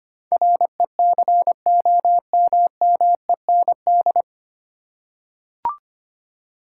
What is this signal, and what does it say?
Morse code, heard as a keyed tone, about 25 wpm, 700 Hz. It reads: RECOMMEND
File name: practice / 25wpm / 9 Letter Words - First 3 Letters Given - 25wpm